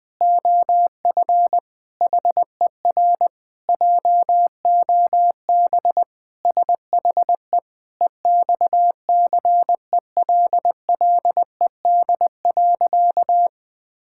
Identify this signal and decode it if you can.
Morse code, heard as a keyed tone, about 20 wpm, 700 Hz. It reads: OF HER JOB SHE EXCELLED.
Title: Sentences with 6-Word Phrases - 20wpm